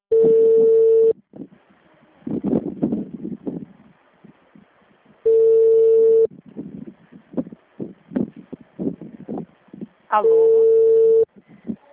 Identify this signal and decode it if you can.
Ringback tone (China, 450 Hz), with speech.
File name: out- -2001- - - .4400